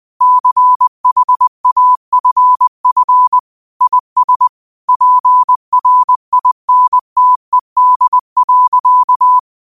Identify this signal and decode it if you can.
Morse code, heard as a keyed tone, about 20 wpm, 1000 Hz. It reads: CHAFF IS PRINTED.
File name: qotdfiles / 20wpm